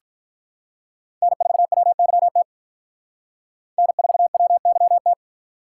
Morse code, heard as a keyed tone, about 44 wpm, 700 Hz. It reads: N4WYT N4WYT